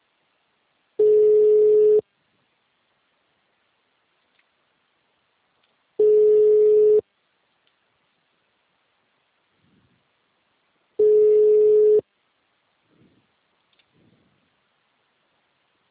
Ringback tone (Europe, 425 Hz).
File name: out- -2030- - - .24